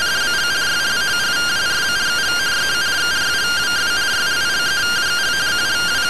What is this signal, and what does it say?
Two-tone teleprinter signal, about 170 Hz shift.